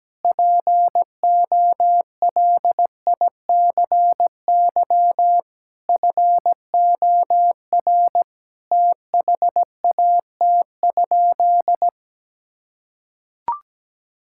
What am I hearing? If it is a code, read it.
Morse code, heard as a keyed tone, about 17 wpm, 700 Hz. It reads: POLICY FOR THAT?
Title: Sentences from Top 1000 Words - Review Part 5 - 17wpm